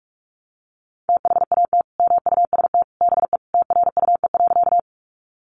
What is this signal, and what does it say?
Morse code, heard as a keyed tone, about 44 wpm, 700 Hz. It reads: THAT MUST BE TRUE.
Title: Sentences from Top 400 Words 45wpm